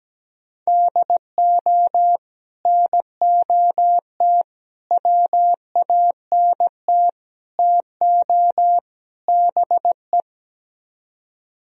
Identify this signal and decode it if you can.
Morse code, heard as a keyed tone, about 17 wpm, 700 Hz. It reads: DO NOT WANT TO BE
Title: Speed-Racing - Sets of 5 Words 17wpm